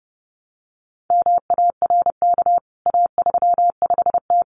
Morse code, heard as a keyed tone, about 30 wpm, 700 Hz. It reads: MARK A35T
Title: CWT Exchange 30wpm